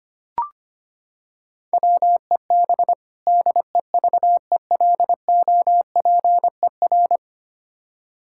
Morse code, heard as a keyed tone, about 25 wpm, 700 Hz. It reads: WEB DEVELOPER
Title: Practice QSO - Exchange 3 - More Info - 25wpm